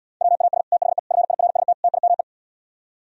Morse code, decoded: WAS THE FIRST TIME

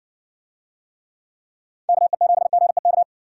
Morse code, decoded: KE7GEK

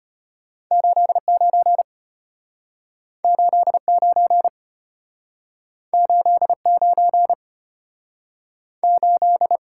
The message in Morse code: 89 89 89 8

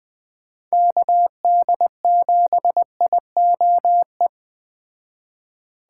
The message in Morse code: KD7IOE

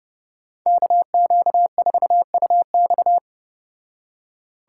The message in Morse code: KQ4UX

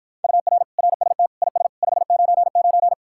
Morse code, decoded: UR RST IS 599

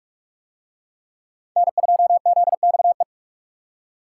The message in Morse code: N2ZXE

